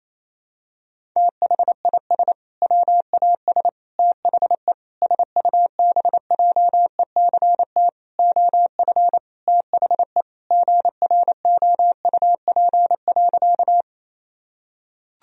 Morse code, heard as a keyed tone, about 28 wpm, 700 Hz. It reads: THIS WAS THE SUBJECT OF THE GROUP.
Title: Sentences from Top 1000 Words - Review Part 1 - 28wpm